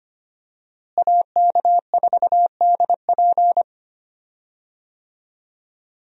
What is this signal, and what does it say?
Morse code, heard as a keyed tone, about 25 wpm, 700 Hz. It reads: AK4DP
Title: USA Call Signs - 5 Characters - Repeated 3x - 25wpm